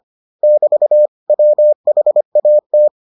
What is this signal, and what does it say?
Morse code, heard as a keyed tone, about 25 wpm, 600 Hz. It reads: <BT> WHAT